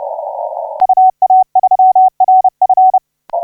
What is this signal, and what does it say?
Morse code, heard as a keyed tone, about 31 wpm, 750 Hz. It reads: UA3RF
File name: A25RJ 1349UTC 15m